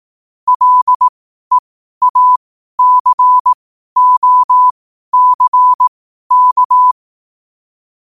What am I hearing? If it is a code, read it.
Morse code, heard as a keyed tone, about 18 wpm, spaced as for 12 wpm, 1000 Hz. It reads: LEACOCK
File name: qotdfiles / 12wpm